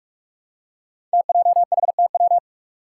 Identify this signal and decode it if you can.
Morse code, heard as a keyed tone, about 44 wpm, 700 Hz. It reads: TJHTW
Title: Five Challenging Letter-Number Combinations 45wpm